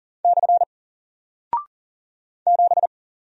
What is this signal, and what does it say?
Morse code, heard as a keyed tone, about 39 wpm, 700 Hz. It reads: / 7
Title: Single Character (Letter:Number:Punctuation:Prosign) 40wpm